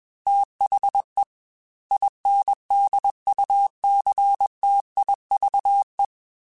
Morse code, heard as a keyed tone, about 21 wpm, 800 Hz. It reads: THE INDUCTIVE